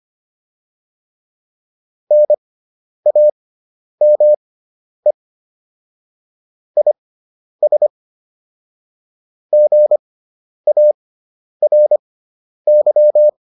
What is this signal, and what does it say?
Morse code, heard as a keyed tone, about 25 wpm, spaced as for 10 wpm, 600 Hz. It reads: NAME IS GARY